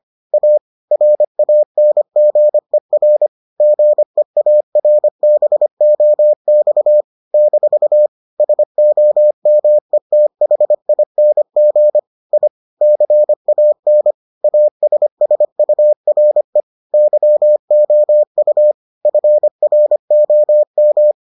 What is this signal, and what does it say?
Morse code, heard as a keyed tone, about 25 wpm, 600 Hz. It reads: A RANGER GEARBOX - SOMETHING I CAN ASSURE YOU FROM